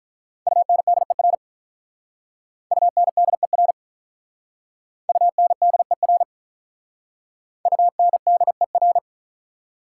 Morse code, heard as a keyed tone, about 44 wpm, 690 Hz. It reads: UNDER UNDER UNDER UNDER
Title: Speed-Racing Prefixes and Suffixes 35wpm